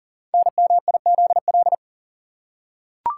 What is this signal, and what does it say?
Morse code, heard as a keyed tone, about 39 wpm, 700 Hz. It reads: NMIZL
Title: Code Groups - 5 Letters - 40wpm